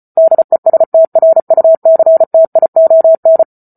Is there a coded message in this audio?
Yes: Morse code from 0.2 to 3.4 s, heard as a keyed tone, about 34 wpm, 650 Hz. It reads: DESTRUCTION